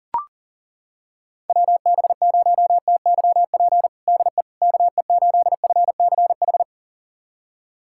Morse code, heard as a keyed tone, about 40 wpm, 700 Hz. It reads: WB0TYP DE KE8FCH
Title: USA Callsign DE Callsign - 6 Characters - 40wpm